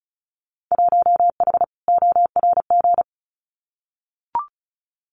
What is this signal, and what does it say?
Morse code, heard as a keyed tone, about 34 wpm, 700 Hz. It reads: 1H ORG
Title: Winter Field Day Exchanges 35wpm